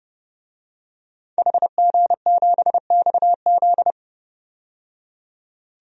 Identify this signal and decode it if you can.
Morse code, heard as a keyed tone, about 29 wpm, 700 Hz. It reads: HG7XZ
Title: DXCC-100 Call Signs - 5 Characters - 30wpm